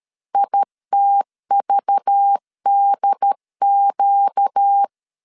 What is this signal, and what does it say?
Morse code, heard as a keyed tone, about 13 wpm, 800 Hz. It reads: ITVDQ
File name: LEZIONI / 6 PKX0 / 6 PKX 12wpm 2024